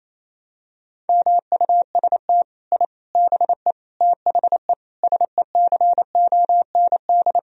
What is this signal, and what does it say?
Morse code, heard as a keyed tone, about 28 wpm, 700 Hz. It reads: MUST I BE THE SECOND
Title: Sentences from Top 1000 Words - Review Part 4 - 28wpm